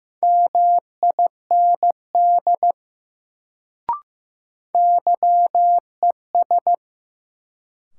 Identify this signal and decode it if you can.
Morse code, heard as a keyed tone, about 15 wpm, 700 Hz. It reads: MIND YES
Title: Sentences from Top 500 Words - New Word Drill - 15wpm